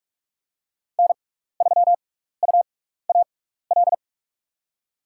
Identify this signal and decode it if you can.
Morse code, heard as a keyed tone, about 45 wpm, spaced as for 15 wpm, 700 Hz. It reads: N3UAL
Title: USA Call Signs - 5 Characters - 6x Character Spacing - 45wpm